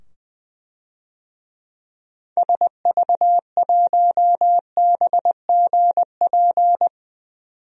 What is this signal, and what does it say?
Morse code, heard as a keyed tone, about 20 wpm, 700 Hz. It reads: SV1BGP
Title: DXCC-100 Call Signs - 6 Characters - 20wpm